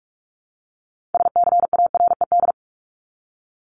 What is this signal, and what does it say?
Morse code, heard as a keyed tone, about 44 wpm, 700 Hz. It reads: SCARED
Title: Top 501-750 Words 45wpm